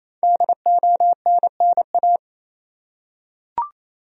Morse code, heard as a keyed tone, about 28 wpm, 700 Hz. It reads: DONNA
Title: USA Common Names - 28wpm